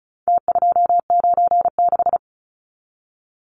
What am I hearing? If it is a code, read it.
Morse code, heard as a keyed tone, about 34 wpm, 700 Hz. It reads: T296
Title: Numbers - Contest Style - 4-Digit 35wpm